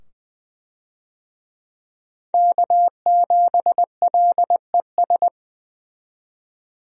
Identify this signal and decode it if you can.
Morse code, heard as a keyed tone, about 20 wpm, 700 Hz. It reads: K7LES
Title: USA Call Signs - 5 Characters - 20wpm